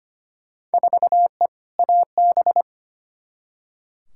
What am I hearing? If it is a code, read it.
Morse code, heard as a keyed tone, about 25 wpm, 700 Hz. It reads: 4E AB
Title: Field Day Exchanges Encore - 25wpm